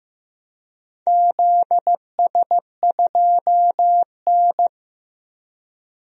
Morse code, heard as a keyed tone, about 15 wpm, 700 Hz. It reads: ZS2N
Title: DXCC-100 Call Signs - 4 Characters - 15wpm